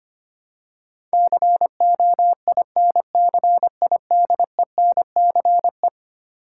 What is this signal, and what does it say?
Morse code, heard as a keyed tone, about 25 wpm, 700 Hz. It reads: COINCIDENCE
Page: morse-code-ninja-practice-25wpm